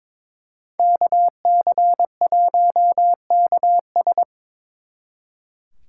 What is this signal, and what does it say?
Morse code, heard as a keyed tone, about 26 wpm, 700 Hz. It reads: KC1KS
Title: Speed Racing - USA Call Signs 22wpm